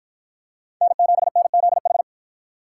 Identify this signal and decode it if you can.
Morse code, heard as a keyed tone, about 50 wpm, 700 Hz. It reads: N7NZH